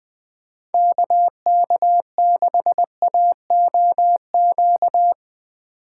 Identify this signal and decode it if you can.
Morse code, heard as a keyed tone, about 20 wpm, 700 Hz. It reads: KK6AOQ